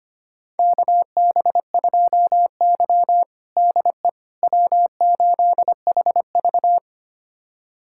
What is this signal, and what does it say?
Morse code, heard as a keyed tone, about 25 wpm, 700 Hz. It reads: KB2Y DE W8HV
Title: USA Callsign DE Callsign - 4 Characters - 25wpm